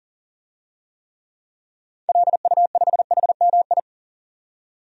Morse code, heard as a keyed tone, about 40 wpm, 700 Hz. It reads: LU5HMI
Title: DXCC-100 Call Signs - 6 Characters - 40wpm